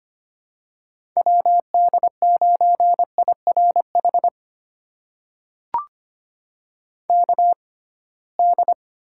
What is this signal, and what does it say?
Morse code, heard as a keyed tone, about 25 wpm, spaced as for 19 wpm, 700 Hz. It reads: WD9IRH KD
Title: USA Call Signs - 6 Characters - 6x Character Spacing - 25wpm